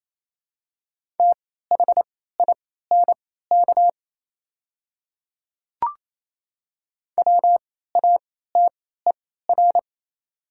Morse code, heard as a keyed tone, about 28 wpm, spaced as for 16 wpm, 700 Hz. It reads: THINK WATER
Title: 3x Character Spacing - Top 200 Words - 28wpm